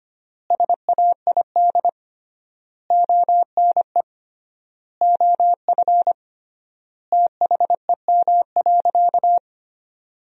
Morse code, heard as a keyed tone, about 25 wpm, 700 Hz. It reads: SAID ONE OF THEM.
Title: 3x Word-Spacing - Sentences from Top 300 Words - 25wpm